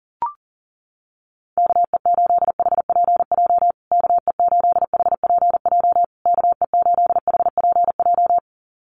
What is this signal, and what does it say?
Morse code, heard as a keyed tone, about 40 wpm, 700 Hz. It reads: KE8HPJ KE8HPJ KE8HPJ
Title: USA Call Signs Repeated 3x 40wpm